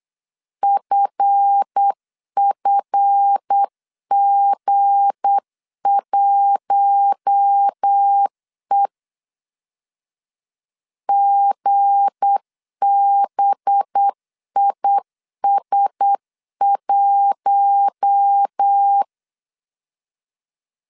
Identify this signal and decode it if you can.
Morse code, heard as a keyed tone, about 8 wpm, 800 Hz. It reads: FFG1E GBIS1
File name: LEZIONI / 7 GF9Z / 7 GF 8wpm 3108